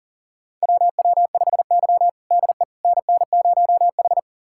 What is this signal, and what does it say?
Morse code, heard as a keyed tone, about 40 wpm, 700 Hz. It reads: WW5Y DE NN0H
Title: USA Callsign DE Callsign - 4 Characters - 40wpm